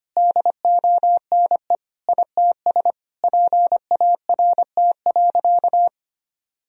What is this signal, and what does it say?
Morse code, heard as a keyed tone, about 25 wpm, 700 Hz. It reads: DONE ITS PART.
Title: Sentences from Top 1000 Words - Review Part 3 - 25wpm